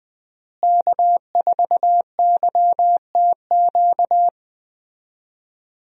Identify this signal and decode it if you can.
Morse code, heard as a keyed tone, about 20 wpm, 700 Hz. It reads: K4YTQ
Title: USA Call Signs - 5 Characters - Repeated 3x - 20wpm